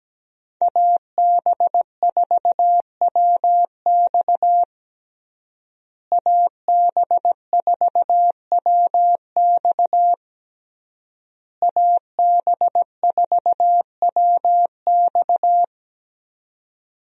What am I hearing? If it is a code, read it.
Morse code, heard as a keyed tone, about 17 wpm, 700 Hz. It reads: AB4WX AB4WX AB4WX
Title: USA Call Signs - 5 Characters - Repeated 3x - 17wpm